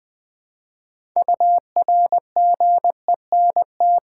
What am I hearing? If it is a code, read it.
Morse code, heard as a keyed tone, about 20 wpm, 700 Hz. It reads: URGENT
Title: 6 Letter Words - First 3 Letters Given - 20wpm